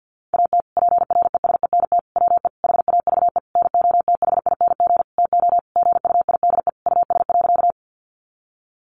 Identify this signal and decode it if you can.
Morse code, heard as a keyed tone, about 50 wpm, 700 Hz. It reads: AT PRESENT WE HAVE NOTHING TO GUIDE US.
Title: Sentences from Top 800 Words 50wpm